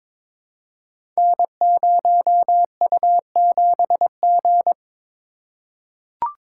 Morse code, heard as a keyed tone, about 22 wpm, 700 Hz. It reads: N0U7G
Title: Code Groups - 5 Letter:Numbers - 22wpm